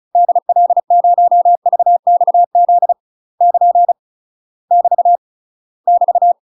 Morse code, heard as a keyed tone, about 36 wpm, 700 Hz. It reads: DL0VXZ <KN> <BT> <BT>
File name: UK MASTER QSOs TRAINING / 35WPM / UK MASTER QSOs 0033